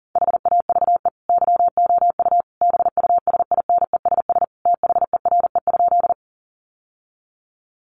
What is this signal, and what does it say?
Morse code, heard as a keyed tone, about 40 wpm, 700 Hz. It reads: HAVE YOU BUSINESS THERE?
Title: Sentences from Top 600 Words 40wpm